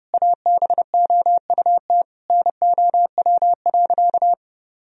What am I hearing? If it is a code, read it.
Morse code, heard as a keyed tone, about 32 wpm, 700 Hz. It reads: ABOUT NOW.
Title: Sentences from Top 1000 Words - Review Part 1 - 30wpm